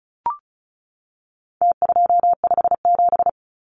Morse code, heard as a keyed tone, about 34 wpm, 700 Hz. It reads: T257
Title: Numbers - Contest Style - 4-Digit 35wpm